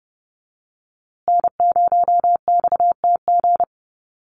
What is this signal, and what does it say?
Morse code, heard as a keyed tone, about 30 wpm, 700 Hz. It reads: N0XTG
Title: USA Call Signs Repeated 3x 30wpm